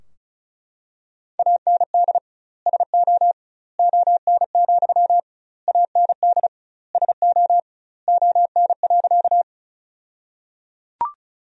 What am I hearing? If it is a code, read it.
Morse code, heard as a keyed tone, about 36 wpm, 700 Hz. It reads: AND SO ON, AND SO ON.
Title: 2x Word-Spacing - Sentences from Top 100 Words - 35wpm